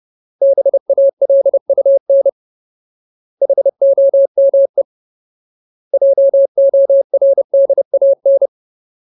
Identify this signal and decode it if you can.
Morse code, heard as a keyed tone, about 30 wpm, 550 Hz. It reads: BALUN HOME JORDAN
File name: Ultimate Word List / 30WPM Ultimate